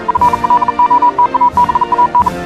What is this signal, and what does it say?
Morse code, heard as a keyed tone, about 40 wpm, 980 Hz. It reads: FLOTABAN
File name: morse hymne(1)